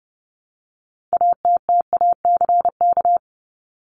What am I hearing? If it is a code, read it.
Morse code, heard as a keyed tone, about 30 wpm, 700 Hz. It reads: ATTACK
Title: Top 1251-1500 Words 30wpm